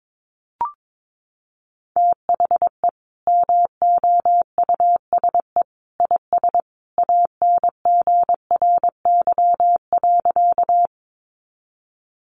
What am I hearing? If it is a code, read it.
Morse code, heard as a keyed tone, about 22 wpm, 700 Hz. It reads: THE MOUSE IS ANGRY.
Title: Easy Phrases 22wpm